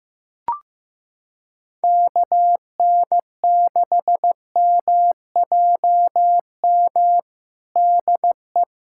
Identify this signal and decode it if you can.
Morse code, heard as a keyed tone, about 15 wpm, 700 Hz. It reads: KN6MJM DE
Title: USA Callsign DE Callsign - 6 Characters - 15wpm